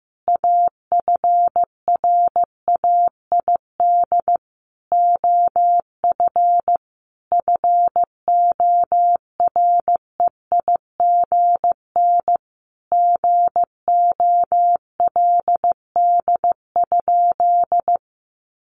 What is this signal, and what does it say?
Morse code, heard as a keyed tone, about 15 wpm, 700 Hz. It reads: AFRAID OF FOREIGN GOLD?
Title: Sentences from Top 1500 Words 15wpm